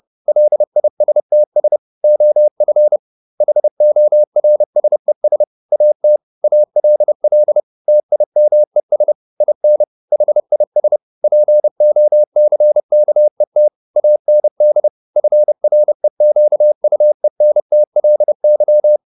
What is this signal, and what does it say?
Morse code, heard as a keyed tone, about 30 wpm, 600 Hz. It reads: LISTS OF HORSES AT ALL TIMES IN HIS POCKET AND FREQUENTLY
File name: the old man and the sea / the old man and the sea 1 30wpm